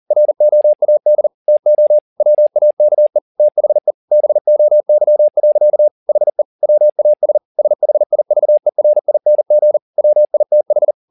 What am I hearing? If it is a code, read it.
Morse code, heard as a keyed tone, about 40 wpm, 600 Hz. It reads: ROAD TO WAKE THE BOY. HE WAS SHIVERING WITH